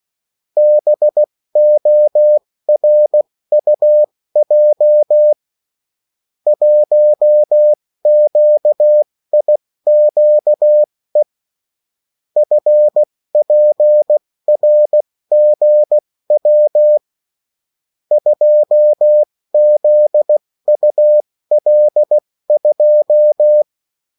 Morse code, heard as a keyed tone, about 16 wpm, spaced as for 14 wpm, 600 Hz. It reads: BORUJ 1QIQE FPRGW 2ZUL2
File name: morse mp3 / revision9-vitesse 14 mots